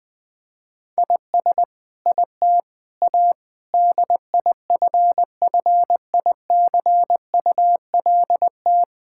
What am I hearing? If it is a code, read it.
Morse code, heard as a keyed tone, about 20 wpm, 700 Hz. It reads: IS IT A DIFFICULT
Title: Sentences from Top 1000 Words - Review Part 6 - 20wpm